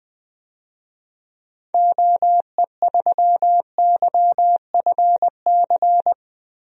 Morse code, heard as a keyed tone, about 20 wpm, 700 Hz. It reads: OE3YFC
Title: DXCC-100 Call Signs - 6 Characters - Repeated 2x - 20wpm